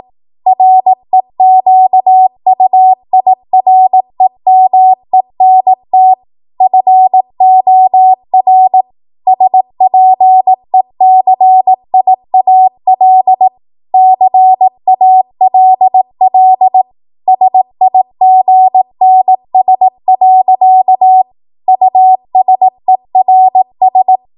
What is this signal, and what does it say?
Morse code, heard as a keyed tone, about 18 wpm, 750 Hz. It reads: REQUIREMENT FOR SPECIAL CALL SIGNS. USERS
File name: Morse / 18WPM